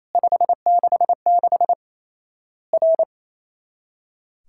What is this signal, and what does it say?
Morse code, heard as a keyed tone, about 27 wpm, 700 Hz. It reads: 566 R
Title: Worked All Europe DX Contest - QTC Exchange - 28wpm